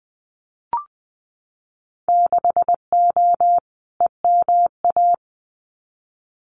Morse code, heard as a keyed tone, about 20 wpm, 700 Hz. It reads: 6O EMA
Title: Winter Field Day Exchanges Encore 20wpm